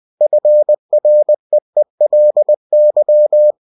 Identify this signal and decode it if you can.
Morse code, heard as a keyed tone, about 20 wpm, 600 Hz. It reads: FREELY